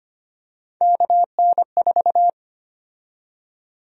Morse code, heard as a keyed tone, about 25 wpm, 700 Hz. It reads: KN4
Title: USA Call Sign Prefixes - 25wpm